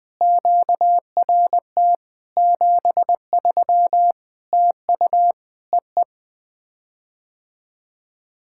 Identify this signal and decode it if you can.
Morse code, heard as a keyed tone, about 20 wpm, 700 Hz. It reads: QRT 73 TU EE
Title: POTA QSO Practice - 20wpm